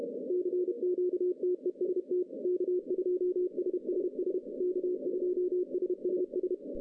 Morse code, heard as a keyed tone, about 32 wpm, 370 Hz. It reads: CQTEST K2SSS K2SSS
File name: 2015 cq ww cw / 2015 cqwww k5tr - 0-41